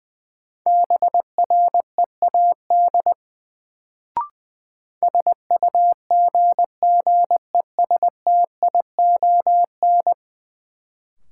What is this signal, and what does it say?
Morse code, heard as a keyed tone, about 20 wpm, 700 Hz. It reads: BREAD SUGGESTION
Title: Sentences from Top 1600 Words - New Word Drill - 20wpm